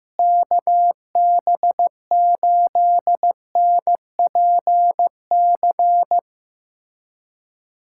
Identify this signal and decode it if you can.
Morse code, heard as a keyed tone, about 15 wpm, 700 Hz. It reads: KB8NPC
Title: USA Callsign DE Callsign - 6 Characters - 15wpm